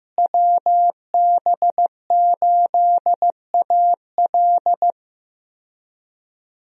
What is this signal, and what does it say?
Morse code, heard as a keyed tone, about 15 wpm, 700 Hz. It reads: WB8AL
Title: USA Call Signs - 5 Characters - Repeated 2x - 15wpm